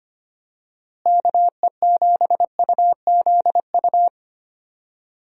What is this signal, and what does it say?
Morse code, heard as a keyed tone, about 25 wpm, 700 Hz. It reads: KE7UZU
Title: USA Call Signs - 6 Characters - Repeated 3x - 25wpm